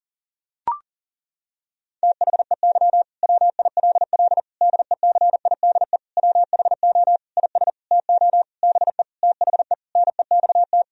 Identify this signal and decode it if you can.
Morse code, heard as a keyed tone, about 40 wpm, 700 Hz. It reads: THEY WILL DECIDE WHO IS TO BE THE NEXT
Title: Sentences from Top 1000 Words - Review Part 3 - 40wpm